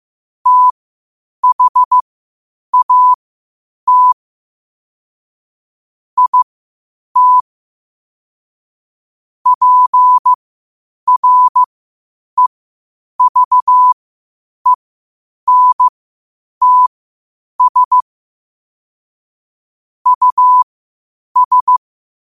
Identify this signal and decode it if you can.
Morse code, heard as a keyed tone, about 15 wpm, spaced as for 8 wpm, 1000 Hz. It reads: THAT IT PREVENTS US
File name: qotdfiles / 8wpm